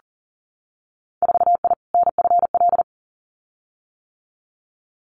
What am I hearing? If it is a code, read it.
Morse code, heard as a keyed tone, about 42 wpm, 700 Hz. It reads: 4I NFL
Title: Winter Field Day Exchanges Encore 40wpm